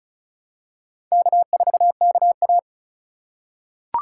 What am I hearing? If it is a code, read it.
Morse code, heard as a keyed tone, about 36 wpm, 700 Hz. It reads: K4KA
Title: USA Call Signs - 2x Character Spacing - 35wpm